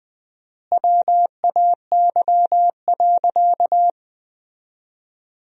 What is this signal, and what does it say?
Morse code, heard as a keyed tone, about 20 wpm, 700 Hz. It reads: WAY.
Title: 4x Word-Spacing - Sentences from Top 100 Words - 20wpm